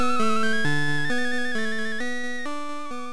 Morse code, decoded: none